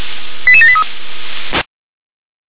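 Five-tone selective call (ZVEI), heard as quick runs of five tones, tones of about 70 ms. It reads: 88783
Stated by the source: ZVEI five-tone call 88150 received by SDR.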